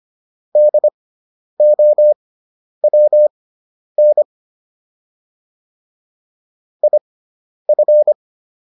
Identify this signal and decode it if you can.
Morse code, heard as a keyed tone, about 25 wpm, spaced as for 10 wpm, 600 Hz. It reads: DOWN IF